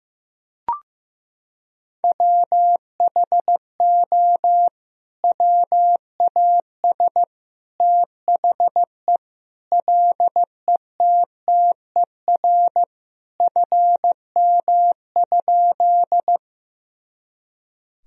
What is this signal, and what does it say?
Morse code, heard as a keyed tone, about 14 wpm, 700 Hz. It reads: WHO WAS THE LETTER FM?
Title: Sentences from Top 300 Words and Abbreviations 15wpm